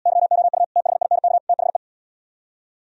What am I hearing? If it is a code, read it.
Morse code, heard as a keyed tone, about 73 wpm, 700 Hz. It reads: YOU THINK THE